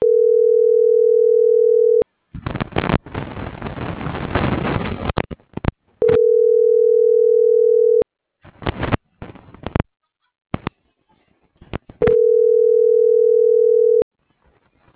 Ringback tone (North America), 440 + 480 Hz.